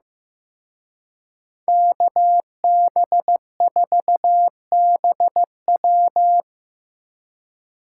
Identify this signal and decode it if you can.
Morse code, heard as a keyed tone, about 18 wpm, 700 Hz. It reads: KB4BW
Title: Speed Racing - USA 2x2 Call Signs 15wpm